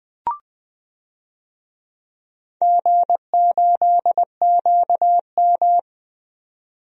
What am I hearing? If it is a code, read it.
Morse code, heard as a keyed tone, about 20 wpm, 700 Hz. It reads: G8QM
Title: DXCC-100 Call Signs - 4 Characters - 20wpm